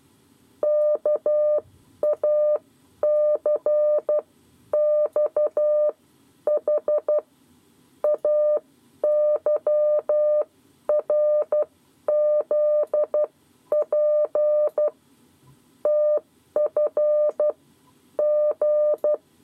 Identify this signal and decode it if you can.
Morse code, heard as a keyed tone, about 12 wpm, spaced as for 10 wpm, 600 Hz. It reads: KACXH AYRZP TF G
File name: 7 Lezione cw 2